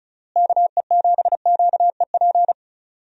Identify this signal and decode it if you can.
Morse code, heard as a keyed tone, about 36 wpm, 700 Hz. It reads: KE7QEP
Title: USA Call Signs - 6 Characters - Repeated 3x - 35wpm